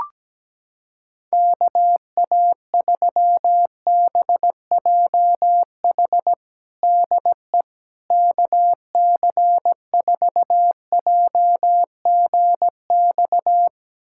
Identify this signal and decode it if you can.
Morse code, heard as a keyed tone, about 17 wpm, 700 Hz. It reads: KA3BJH DE KC4JGX
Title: USA Callsign DE Callsign - 6 Characters - 17wpm